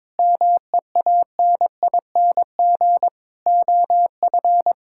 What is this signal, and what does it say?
Morse code, heard as a keyed tone, about 22 wpm, 700 Hz. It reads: MEANING OF
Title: Sentences from Top 1000 Words - Review Part 6 - 22wpm